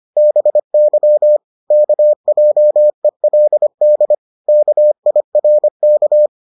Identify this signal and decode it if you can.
Morse code, heard as a keyed tone, about 25 wpm, 600 Hz. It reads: BY KJELD KIRK